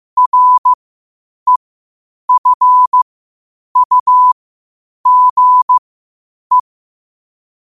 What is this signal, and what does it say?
Morse code, heard as a keyed tone, about 15 wpm, spaced as for 8 wpm, 1000 Hz. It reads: REFUGE
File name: qotdfiles / 8wpm